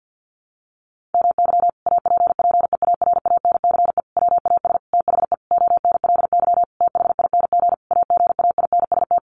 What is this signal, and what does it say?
Morse code, heard as a keyed tone, about 49 wpm, 700 Hz. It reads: MY APPEARANCE WAS THE ONLY THING AGAINST
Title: Sentences from Top 1100 Words 50wpm